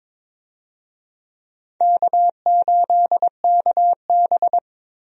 Morse code, heard as a keyed tone, about 22 wpm, 700 Hz. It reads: K8KB
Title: USA Call Signs - 4 Characters - Repeated 2x - 22wpm